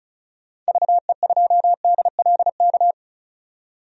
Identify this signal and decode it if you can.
Morse code, heard as a keyed tone, about 34 wpm, 700 Hz. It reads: VE2DLK
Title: DXCC-100 Call Signs - 6 Characters - 35wpm